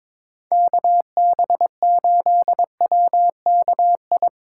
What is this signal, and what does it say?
Morse code, heard as a keyed tone, about 22 wpm, 700 Hz. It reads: KB8WKI
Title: USA Call Signs - 6 Characters - 2x Character Spacing - 22wpm